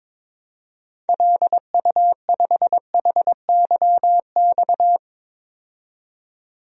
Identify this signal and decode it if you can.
Morse code, heard as a keyed tone, about 22 wpm, 700 Hz. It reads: LU5HYX
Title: DXCC-100 Call Signs - 6 Characters - 22wpm